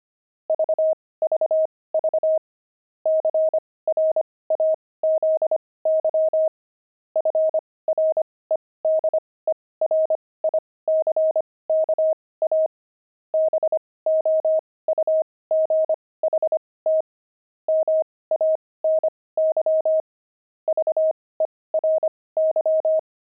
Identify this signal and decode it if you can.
Morse code, heard as a keyed tone, about 25 wpm, spaced as for 18 wpm, 630 Hz. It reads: VVV CRAZY FREDERICKA BOUGHT MANY VERY
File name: pangram-fs-18-cs-25-opal